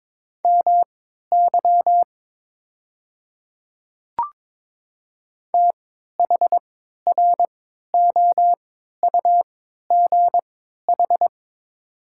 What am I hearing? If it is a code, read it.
Morse code, heard as a keyed tone, about 22 wpm, spaced as for 12 wpm, 700 Hz. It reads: MY THROUGH
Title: 3x Character Spacing - Top 200 Words - 22wpm